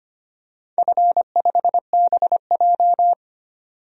Morse code, heard as a keyed tone, about 25 wpm, 700 Hz. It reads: F5BJ
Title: DXCC-100 Call Signs - 4 Characters - 25wpm